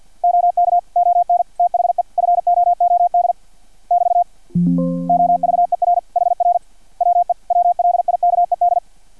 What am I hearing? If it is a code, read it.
Morse code, heard as a keyed tone, about 50 wpm, 700 Hz. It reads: OK ON THE FOOD <BT> OVER HR WE WLIKED
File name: archived / CRSnet